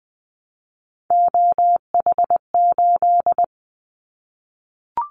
Morse code, heard as a keyed tone, about 20 wpm, 700 Hz. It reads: OH8
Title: Callsign Prefixes for Top 25 DXCC - 20wpm